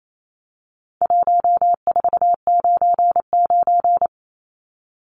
Morse code, heard as a keyed tone, about 28 wpm, 700 Hz. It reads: 1499
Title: Numbers - 4-Digit 28wpm